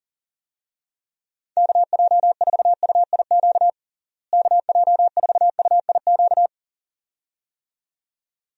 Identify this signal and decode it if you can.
Morse code, heard as a keyed tone, about 40 wpm, 700 Hz. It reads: KJ4UIQ KJ4UIQ
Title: USA Call Signs - 6 Characters - Repeated 2x - 40wpm